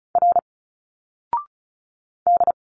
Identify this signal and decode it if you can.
Morse code, heard as a keyed tone, about 36 wpm, 700 Hz. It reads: R D
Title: Single Letters 35wpm